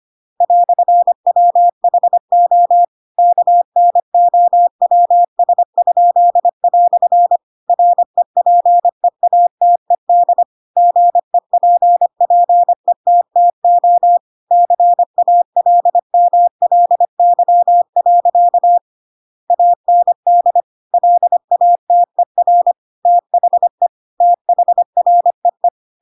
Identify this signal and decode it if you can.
Morse code, heard as a keyed tone, about 25 wpm, 700 Hz. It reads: "WHO KNOWS?" REPEATED GEPPETTO CALMLY. AND LATER THE THREE
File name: Pinocchio / Pinocchio ch7 25 25 ik1hge